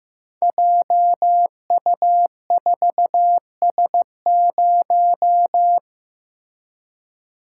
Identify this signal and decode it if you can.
Morse code, heard as a keyed tone, about 15 wpm, 700 Hz. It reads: JU4S0
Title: Code Groups - 5 Letter:Numbers - 15wpm